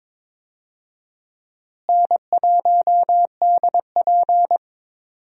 Morse code, heard as a keyed tone, about 22 wpm, 700 Hz. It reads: N1DP
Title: USA Call Signs - 4 Characters - Repeated 2x - 22wpm